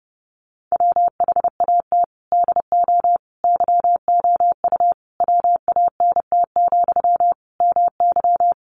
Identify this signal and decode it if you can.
Morse code, heard as a keyed tone, about 30 wpm, 700 Hz. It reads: WHAT DO YOU WANT, MY
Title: Sentences from Top 1000 Words 30wpm